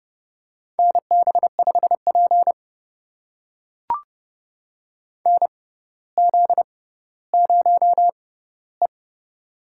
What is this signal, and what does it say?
Morse code, heard as a keyed tone, about 30 wpm, spaced as for 15 wpm, 690 Hz. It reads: NB5P NZ0E
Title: USA Call Signs - 4 Characters - 6x Character Spacing - 30wpm